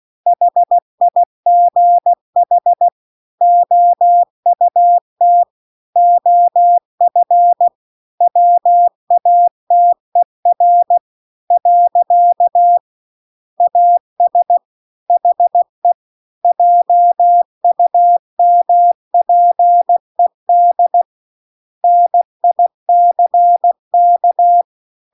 Morse code, heard as a keyed tone, about 16 wpm, 700 Hz. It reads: HIGH OUT OF WATER . AS HE JUMPED NICK
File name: CW-The-Big-Twoheaded-River-2-16WPM